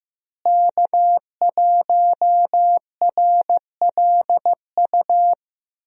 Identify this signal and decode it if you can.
Morse code, heard as a keyed tone, about 15 wpm, 700 Hz. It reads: K1RLU
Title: USA Call Signs - 5 Characters - Repeated 2x - 15wpm